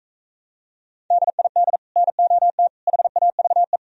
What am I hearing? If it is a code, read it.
Morse code, heard as a keyed tone, about 42 wpm, 700 Hz. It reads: DID NOT HAVE